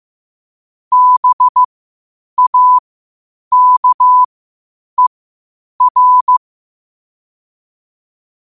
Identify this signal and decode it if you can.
Morse code, heard as a keyed tone, about 15 wpm, spaced as for 8 wpm, 1000 Hz. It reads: BAKER
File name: qotdfiles / 8wpm